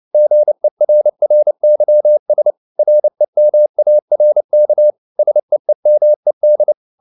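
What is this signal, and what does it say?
Morse code, heard as a keyed tone, about 29 wpm, 600 Hz. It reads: GERRYS REMARK SEEMED
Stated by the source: Morse code at 29 wpm, sending Ken Follett's novel Never in English, from CW med Gnister.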